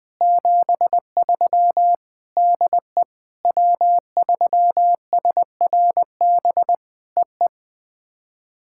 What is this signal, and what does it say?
Morse code, heard as a keyed tone, about 20 wpm, 700 Hz. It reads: 73 DE W3SRB EE
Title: POTA QSO Practice - 20wpm